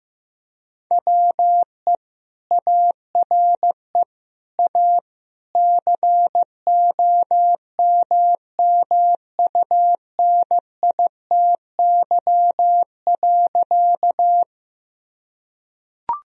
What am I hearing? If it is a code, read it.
Morse code, heard as a keyed tone, about 15 wpm, 700 Hz. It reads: WE ARE A COMMUNITY.
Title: Sentences from Top 1000 Words - Review Part 4 - 15wpm